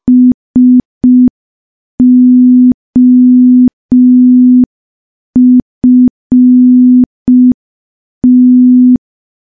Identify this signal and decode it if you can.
Morse code, heard as a keyed tone, about 5 wpm, 250 Hz. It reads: SOFT
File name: morse / morse-training-05WPM